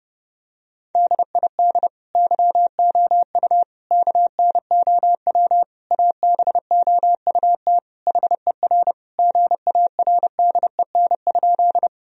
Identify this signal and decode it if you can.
Morse code, heard as a keyed tone, about 30 wpm, 700 Hz. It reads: DID YOU KNOW ABOUT HER GARDEN?
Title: Sentences from Top 1000 Words - Review Part 4 - 30wpm